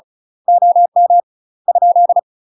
Morse code, heard as a keyed tone, about 36 wpm, 700 Hz. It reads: OM ?
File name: UK MASTER QSOs TRAINING / 35WPM / UK MASTER QSOs 0031